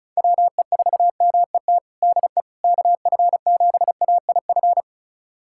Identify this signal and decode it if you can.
Morse code, heard as a keyed tone, about 34 wpm, 700 Hz. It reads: WE4MET DE KF7AIF
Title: USA Callsign DE Callsign - 6 Characters - 35wpm